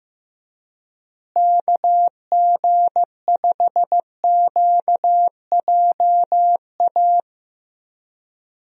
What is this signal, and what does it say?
Morse code, heard as a keyed tone, about 15 wpm, 700 Hz. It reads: KG5QJA
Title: USA Call Signs - 6 Characters - Repeated 3x - 15wpm